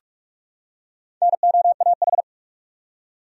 Morse code, heard as a keyed tone, about 46 wpm, 700 Hz. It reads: NOAH
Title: USA Common Names - Max 5 Letters - 45wpm